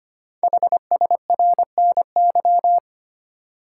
Morse code, heard as a keyed tone, about 25 wpm, 700 Hz. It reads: HSRNY
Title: Code Groups - 5 Letters - 25wpm